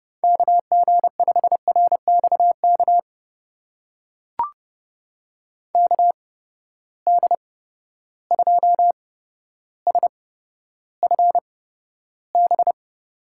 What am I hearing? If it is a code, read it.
Morse code, heard as a keyed tone, about 30 wpm, spaced as for 13 wpm, 700 Hz. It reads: KG5RXK KD2SFB
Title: USA Call Signs - 6 Characters - 8x Character Spacing - 30wpm